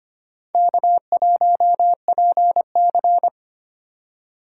Morse code, heard as a keyed tone, about 25 wpm, 700 Hz. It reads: K1PC
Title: USA Call Signs - 4 Characters - 2x Character Spacing - 25wpm